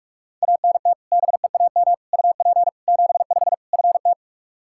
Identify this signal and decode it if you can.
Morse code, heard as a keyed tone, about 46 wpm, 700 Hz. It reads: ANT BEAM UP 75 FT
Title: Practice QSO - Exchange 2 - Info - 45wpm